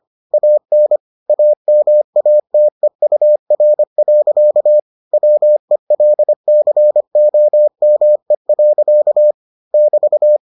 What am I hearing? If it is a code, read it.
Morse code, heard as a keyed tone, about 25 wpm, 600 Hz. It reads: AN AMATEUR. WELCOME. <BT>